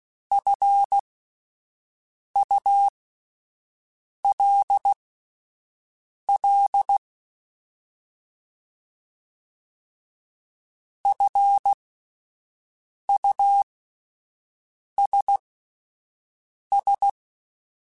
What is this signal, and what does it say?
Morse code, heard as a keyed tone, about 16 wpm, spaced as for 5 wpm, 800 Hz. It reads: FULL FUSS
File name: Alphabetical / Alphabetical mp3s / Fs @5